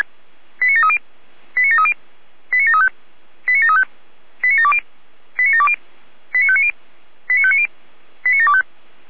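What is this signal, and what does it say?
Five-tone selective call (ZVEI), heard as quick runs of five tones, tones of about 70 ms. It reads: 89730 89730 89735 89735 89720 89720 89590 89590 89725
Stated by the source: ZVEI five-tone selective call received on an SDR.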